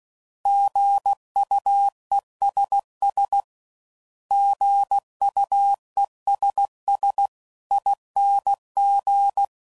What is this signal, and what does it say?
Morse code, heard as a keyed tone, about 16 wpm, 800 Hz. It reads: GUESS GUESS ING
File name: Prefixes / By First Letter / Gs